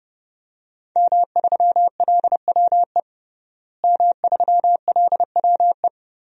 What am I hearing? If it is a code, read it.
Morse code, heard as a keyed tone, about 30 wpm, 700 Hz. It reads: M3LWE M3LWE
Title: DXCC-100 Call Signs - 5 Characters - Repeated 2x - 30wpm